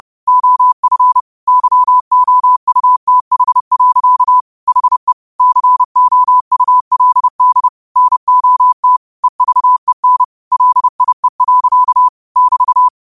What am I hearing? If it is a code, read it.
Morse code, heard as a keyed tone, about 30 wpm, 1000 Hz. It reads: OF YOUTH. HE COULD NOT EVEN LIE. <BT>
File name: qotdfiles / 30wpm